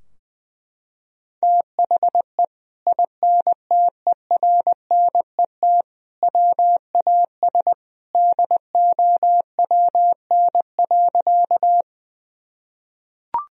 Morse code, heard as a keyed tone, about 21 wpm, 700 Hz. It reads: THE INTERNET WAS DOWN.
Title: Sentences from Top 1000 Words - Review Part 4 - 20wpm